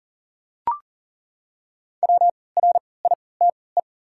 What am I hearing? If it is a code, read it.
Morse code, heard as a keyed tone, about 40 wpm, spaced as for 23 wpm, 700 Hz. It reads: WRITE
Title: 3x Character Spacing - Top 300 Words - 40wpm